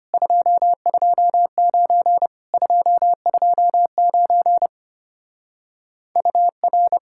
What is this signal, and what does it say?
Morse code, heard as a keyed tone, about 33 wpm, 700 Hz. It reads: 229 229 UR